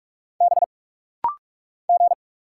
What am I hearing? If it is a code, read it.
Morse code, heard as a keyed tone, about 46 wpm, 700 Hz. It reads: B G
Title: Single Letters - Rapid-Fire 45wpm